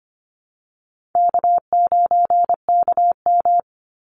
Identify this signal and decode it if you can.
Morse code, heard as a keyed tone, about 25 wpm, 700 Hz. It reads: K9KM
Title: Call Signs Repeated 3x - 25wpm